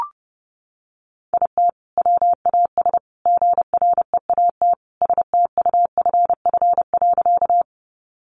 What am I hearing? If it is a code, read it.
Morse code, heard as a keyed tone, about 30 wpm, 700 Hz. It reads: IT WAS GREAT STUFF.
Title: Sentences from Top 1400 Words 30wpm